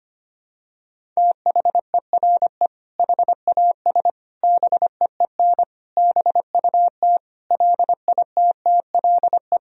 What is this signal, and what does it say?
Morse code, heard as a keyed tone, about 25 wpm, 700 Hz. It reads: THERE HAS BEEN BUT LITTLE
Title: Sentences from Top 1000 Words - Review Part 1 - 25wpm